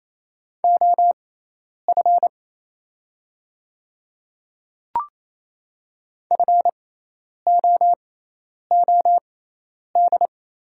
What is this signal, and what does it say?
Morse code, heard as a keyed tone, about 28 wpm, spaced as for 10 wpm, 700 Hz. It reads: OF FOOD
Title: 6x Character Spacing - Top 300 Words - 28wpm